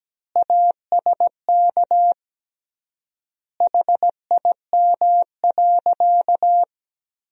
Morse code, heard as a keyed tone, about 17 wpm, 700 Hz. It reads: ASK HIM.
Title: 3x Word-Spacing - Sentences from Top 300 Words - 17wpm